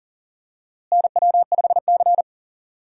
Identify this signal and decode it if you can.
Morse code, heard as a keyed tone, about 39 wpm, 700 Hz. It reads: NW5C
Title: USA Call Signs - 4 Characters - Repeated 3x - 40wpm